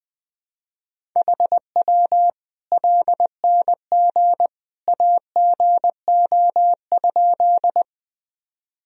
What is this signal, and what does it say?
Morse code, heard as a keyed tone, about 20 wpm, 700 Hz. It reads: HW LNG AGO?
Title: Sentences from Top 500 Words and Abbreviations 20wpm